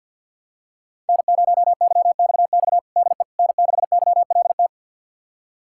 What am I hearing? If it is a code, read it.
Morse code, heard as a keyed tone, about 50 wpm, 700 Hz. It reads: N0YXX DE N6YLT
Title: USA Callsign DE Callsign - 5 Characters - 50wpm